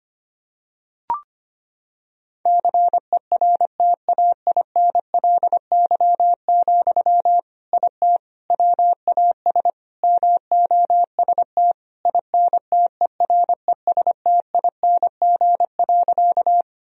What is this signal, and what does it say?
Morse code, heard as a keyed tone, about 25 wpm, 700 Hz. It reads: CERTAINLY, IT WAS MOST INTERESTING.
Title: Sentences from Top 1000 Words - Review Part 1 - 25wpm